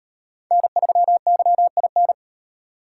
Morse code, heard as a keyed tone, about 33 wpm, 700 Hz. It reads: N3YIN